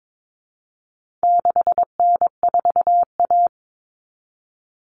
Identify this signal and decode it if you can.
Morse code, heard as a keyed tone, about 22 wpm, 700 Hz. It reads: 6N4A
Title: Common Cut Numbers - 4-Digit 22wpm